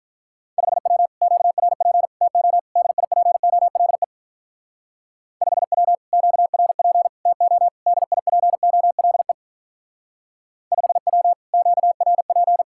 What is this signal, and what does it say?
Morse code, heard as a keyed tone, about 45 wpm, 700 Hz. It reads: 5W QRP TO DIPOLE 5W QRP TO DIPOLE 5W QRP